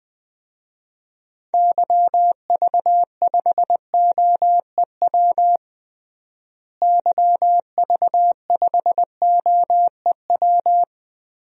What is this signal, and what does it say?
Morse code, heard as a keyed tone, about 20 wpm, 700 Hz. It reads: YV5OEW YV5OEW
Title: DXCC-100 Call Signs - 6 Characters - Repeated 2x - 20wpm